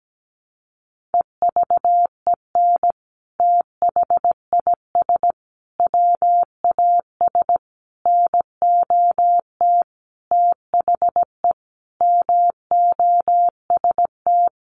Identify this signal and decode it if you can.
Morse code, heard as a keyed tone, about 17 wpm, 700 Hz. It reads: EVEN THIS WAS NOT THE MOST
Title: Sentences from Top 1100 Words 17wpm